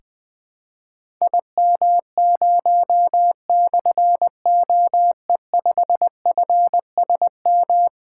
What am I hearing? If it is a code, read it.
Morse code, heard as a keyed tone, about 20 wpm, 700 Hz. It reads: IM0/OE5FSM
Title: Difficult Call Signs 20wpm